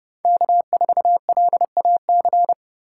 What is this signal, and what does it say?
Morse code, heard as a keyed tone, about 39 wpm, 700 Hz. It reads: K4LAC